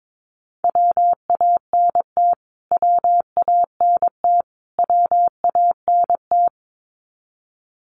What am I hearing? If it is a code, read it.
Morse code, heard as a keyed tone, about 22 wpm, 700 Hz. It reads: WANT WANT WANT
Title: Top 500 Words Repeated 22wpm